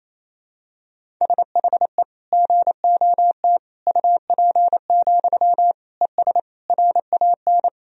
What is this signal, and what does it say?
Morse code, heard as a keyed tone, about 27 wpm, 700 Hz. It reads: SHE GOT UP, ES RAN
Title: Sentences from Top 500 Words and Abbreviations 28wpm